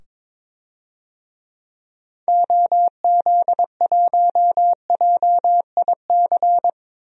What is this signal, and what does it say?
Morse code, heard as a keyed tone, about 22 wpm, 700 Hz. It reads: OZ1JIC
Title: DXCC-100 Call Signs - 6 Characters - 22wpm